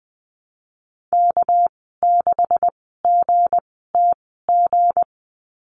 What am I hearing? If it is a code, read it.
Morse code, heard as a keyed tone, about 20 wpm, spaced as for 14 wpm, 700 Hz. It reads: K6GTG
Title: 231 Lesson 37g - Letter x - Call Signs 20wpm